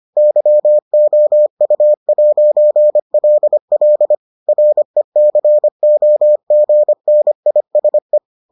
Morse code, heard as a keyed tone, about 25 wpm, 600 Hz. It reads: YOU'LL RECOGNISE